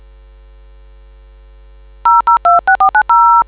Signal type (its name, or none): DTMF